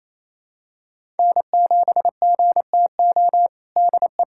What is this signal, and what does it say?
Morse code, heard as a keyed tone, about 28 wpm, 700 Hz. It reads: N7GTO DE
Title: USA Callsign DE Callsign - 5 Characters - 28wpm